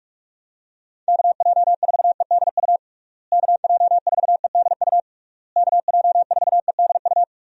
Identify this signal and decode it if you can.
Morse code, heard as a keyed tone, about 46 wpm, 700 Hz. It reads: KJ4EDU KJ4EDU KJ4EDU
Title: USA Call Signs - 6 Characters - Repeated 3x - 45wpm